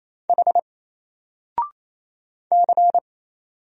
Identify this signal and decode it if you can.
Morse code, heard as a keyed tone, about 27 wpm, 700 Hz. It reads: H C
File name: Single-Letter-Number-28wpm